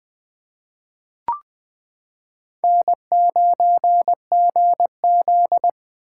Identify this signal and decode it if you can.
Morse code, heard as a keyed tone, about 20 wpm, 700 Hz. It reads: N9GZ
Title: 2023 POTA Hunter Call Signs - 20wpm